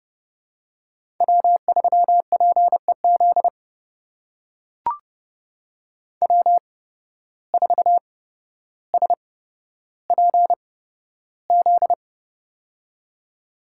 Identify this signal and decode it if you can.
Morse code, heard as a keyed tone, about 30 wpm, spaced as for 13 wpm, 700 Hz. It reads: W3PEZ W4SPZ
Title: USA Call Signs - 5 Characters - 8x Character Spacing - 30wpm